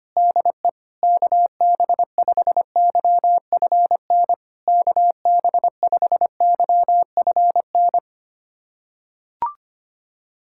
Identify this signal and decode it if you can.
Morse code, heard as a keyed tone, about 25 wpm, 700 Hz. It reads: DE KB5YFN KB5YFN
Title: SOTA QSO Practice 25wpm